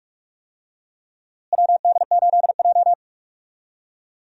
Morse code, heard as a keyed tone, about 42 wpm, 700 Hz. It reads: WD8J
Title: USA Call Signs - 4 Characters - Repeated 3x - 45wpm